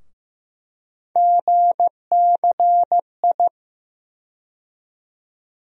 Morse code, heard as a keyed tone, about 15 wpm, 700 Hz. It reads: GCI
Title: USA Call Sign Postfixes - 3 Letters 15wpm